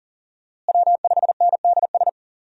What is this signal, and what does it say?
Morse code, heard as a keyed tone, about 40 wpm, 700 Hz. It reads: W5NDS
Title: USA Call Signs - 5 Characters - Repeated 2x - 40wpm